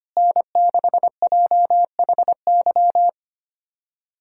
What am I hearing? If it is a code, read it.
Morse code, heard as a keyed tone, about 25 wpm, 700 Hz. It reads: N6JHY
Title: USA Callsign DE Callsign - 5 Characters - 25wpm